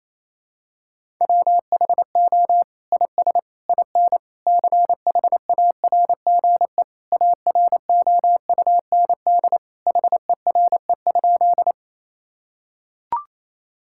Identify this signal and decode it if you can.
Morse code, heard as a keyed tone, about 27 wpm, 700 Hz. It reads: WHO IS IN CHARGE AROUND HERE?
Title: Sentences from Top 1000 Words - Review Part 6 - 28wpm